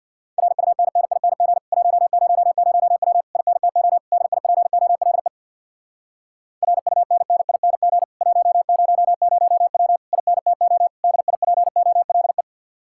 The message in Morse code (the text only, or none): RUNNING 100W INTO DIPOLE RUNNING 100W INTO DIPOLE